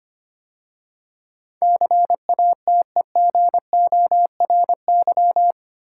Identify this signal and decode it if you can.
Morse code, heard as a keyed tone, about 25 wpm, 690 Hz. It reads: CATEGORY
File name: 8 Letter Words - First 3 Letters Given - 25wpm